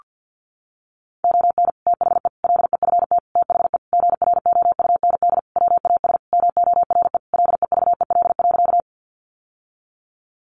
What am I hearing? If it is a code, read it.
Morse code, heard as a keyed tone, about 50 wpm, 700 Hz. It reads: ON THE LEFT THE GROUND WAS MORE LEVEL.
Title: Sentences from Top 700 Words 50wpm